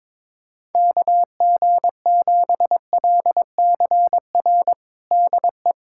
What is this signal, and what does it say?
Morse code, heard as a keyed tone, about 22 wpm, 700 Hz. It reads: KG7LCR DE
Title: USA Callsign DE Callsign 22wpm